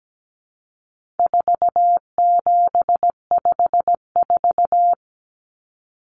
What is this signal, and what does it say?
Morse code, heard as a keyed tone, about 17 wpm, 700 Hz. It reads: 4754
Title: Common Cut Numbers - 4-Digit 17wpm